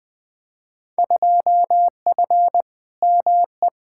Morse code, heard as a keyed tone, about 20 wpm, 700 Hz. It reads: 2F ME
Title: Field Day Exchanges - 20wpm